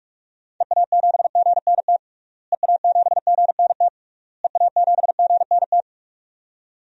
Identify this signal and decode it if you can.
Morse code, heard as a keyed tone, about 41 wpm, 700 Hz. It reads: EA7GNT EA7GNT EA7GNT
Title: DXCC-100 Call Signs - 6 Characters - Repeated 3x - 45wpm